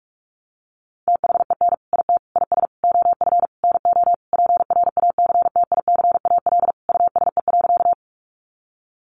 Morse code, heard as a keyed tone, about 44 wpm, 700 Hz. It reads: THEN IT IS OF NO PRACTICAL USE.
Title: Sentences from Top 1400 Words 45wpm